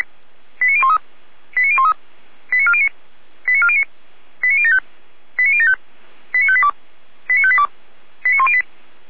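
Five-tone selective call (ZVEI), heard as quick runs of five tones, tones of about 70 ms. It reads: 89013 89013 89409 89409 89075 89075 89582 89582 89108
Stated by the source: ZVEI five-tone selective call received with an SDR.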